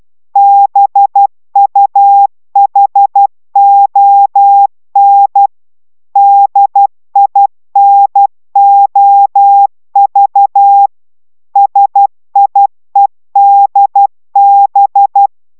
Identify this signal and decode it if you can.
Morse code, heard as a keyed tone, about 12 wpm, 800 Hz. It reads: BUHON DINOV SIEDB